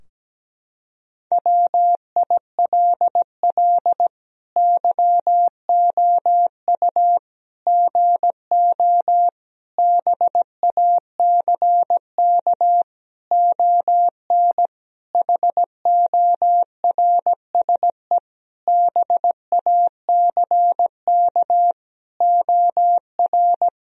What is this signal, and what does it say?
Morse code, heard as a keyed tone, about 17 wpm, 700 Hz. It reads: WILL YOU GO BACK ON HORSE BACK OR
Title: Sentences from Top 1000 Words - Review Part 3 - 17wpm